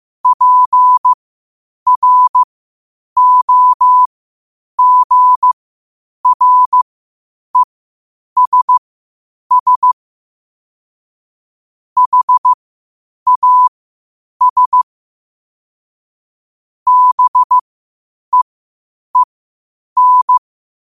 Morse code, heard as a keyed tone, about 15 wpm, spaced as for 8 wpm, 1000 Hz. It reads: PROGRESS HAS BEEN
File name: qotdfiles / 8wpm